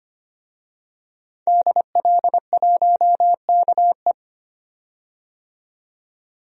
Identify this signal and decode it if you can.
Morse code, heard as a keyed tone, about 25 wpm, 700 Hz. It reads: DL1KE
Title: DXCC-100 Call Signs - 5 Characters - 25wpm